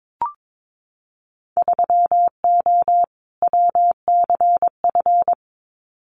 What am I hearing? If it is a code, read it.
Morse code, heard as a keyed tone, about 22 wpm, 700 Hz. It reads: 3O WCF
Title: Winter Field Day Exchanges 22wpm